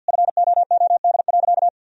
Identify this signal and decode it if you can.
Morse code, heard as a keyed tone, about 52 wpm, 700 Hz. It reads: FOOD.
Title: Sentences from Top 1000 Words - Review Part 3 - 50wpm